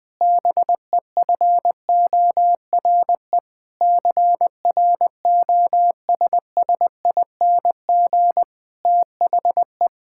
Morse code, heard as a keyed tone, about 19 wpm, 700 Hz. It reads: BEFORE CROSSING THE
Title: Difficult Sentences with Hints - 20wpm